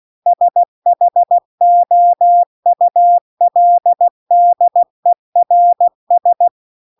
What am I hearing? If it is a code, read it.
Morse code, heard as a keyed tone, about 16 wpm, 700 Hz. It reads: SHOULDERS